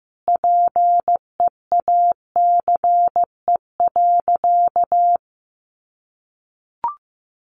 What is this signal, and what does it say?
Morse code, heard as a keyed tone, about 15 wpm, 700 Hz. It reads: PEACE.
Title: Sentences from Top 1500 Words 15wpm